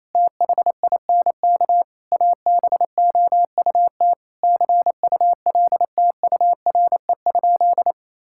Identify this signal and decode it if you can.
Morse code, heard as a keyed tone, about 28 wpm, 700 Hz. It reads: THINK ABOUT CULTURE?
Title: Sentences from Top 1000 Words - Review Part 6 - 28wpm